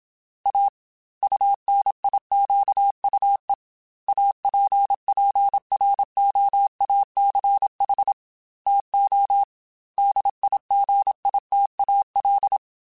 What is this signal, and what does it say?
Morse code, heard as a keyed tone, about 27 wpm, 800 Hz. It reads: A UNIQUE APPROACH TO DIGITAL
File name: Technical Subjects / Technical Subjects-mp3s / Gates Intro @25